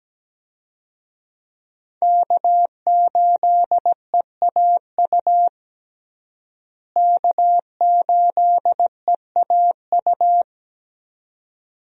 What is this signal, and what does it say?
Morse code, heard as a keyed tone, about 17 wpm, 700 Hz. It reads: K8EAU K8EAU
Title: USA Call Signs - 5 Characters - Repeated 2x - 17wpm